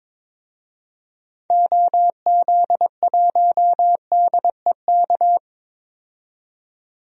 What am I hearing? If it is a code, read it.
Morse code, heard as a keyed tone, about 22 wpm, 700 Hz. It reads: OZ1DEK
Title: DXCC-100 Call Signs - 6 Characters - 22wpm